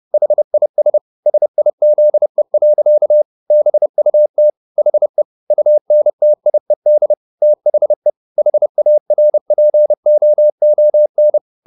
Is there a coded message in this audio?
Yes: Morse code from 0.1 to 11.4 s, heard as a keyed tone, about 30 wpm, 600 Hz. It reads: HIS SIZE. BUT HE UNTIED THE HARPOON